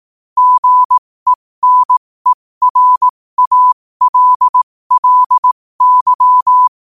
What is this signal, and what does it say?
Morse code, heard as a keyed tone, about 18 wpm, spaced as for 15 wpm, 1000 Hz. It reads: GENERALLY
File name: qotdfiles / 15wpm